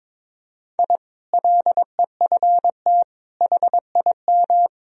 Morse code, heard as a keyed tone, about 22 wpm, 700 Hz. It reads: I LEFT HIM
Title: Sentences from Top 1000 Words - Review Part 1 - 22wpm